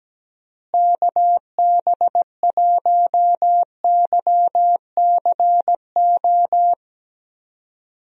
Morse code, heard as a keyed tone, about 17 wpm, 700 Hz. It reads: KB1YCO
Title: USA Call Signs - 6 Characters - Repeated 2x - 17wpm